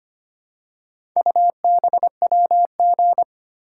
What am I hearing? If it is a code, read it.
Morse code, heard as a keyed tone, about 25 wpm, 700 Hz. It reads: UBWG
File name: practice / 25wpm / Four Letter-Number Combinations - 25wpm